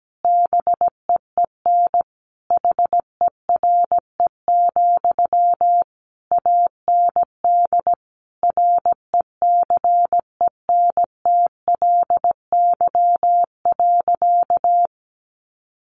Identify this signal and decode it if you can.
Morse code, heard as a keyed tone, about 17 wpm, 700 Hz. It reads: BEEN HERE, AND RECENTLY.
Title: Sentences from Top 1200 Words 17wpm